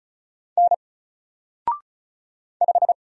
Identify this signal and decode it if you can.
Morse code, heard as a keyed tone, about 36 wpm, 700 Hz. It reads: N 5
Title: Single Letter-Number Alt 35wpm